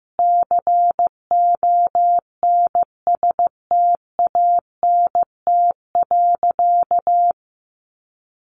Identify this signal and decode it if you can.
Morse code, heard as a keyed tone, about 15 wpm, 700 Hz. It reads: CONSTANT.
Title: Sentences from Top 1500 Words 15wpm